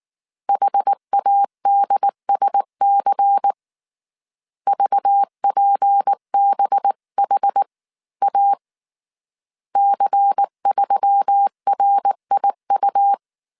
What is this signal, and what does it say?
Morse code, heard as a keyed tone, about 19 wpm, 800 Hz. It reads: HADSC VPBHA C3RIU